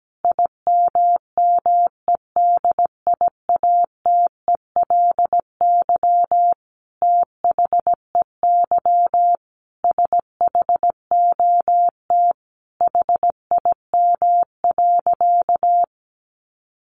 Morse code, heard as a keyed tone, about 17 wpm, 700 Hz. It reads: IMMEDIATELY THEY SHOT HIM.
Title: Sentences from Top 1100 Words 17wpm